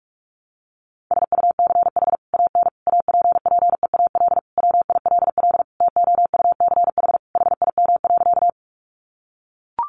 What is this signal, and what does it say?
Morse code, heard as a keyed tone, about 44 wpm, 700 Hz. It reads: SUCH AN APPEAL WILL TOUCH HIM.
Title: Sentences from Top 1300 Words 45wpm